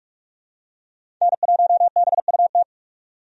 Morse code, heard as a keyed tone, about 46 wpm, 700 Hz. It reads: N1BUT
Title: USA Call Signs - 5 Characters - 45wpm